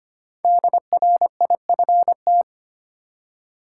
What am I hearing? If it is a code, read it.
Morse code, heard as a keyed tone, about 25 wpm, 700 Hz. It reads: DRIFT
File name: 5 Letter Words - First 3 Letters Given - 25wpm